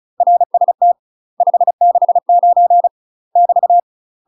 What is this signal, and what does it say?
Morse code, heard as a keyed tone, about 34 wpm, 700 Hz. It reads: RST 569 <BT>
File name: UK MASTER QSOs TRAINING/35WPM